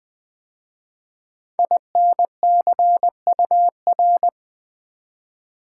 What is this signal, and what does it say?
Morse code, heard as a keyed tone, about 20 wpm, 700 Hz. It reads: INCUR
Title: 5 Letter Words - First 3 Letters Given - 20wpm